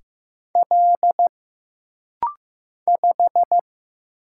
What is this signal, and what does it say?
Morse code, heard as a keyed tone, about 15 wpm, 700 Hz. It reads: L 5
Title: Single Letter-Number Alt - Rapid-Fire 15wpm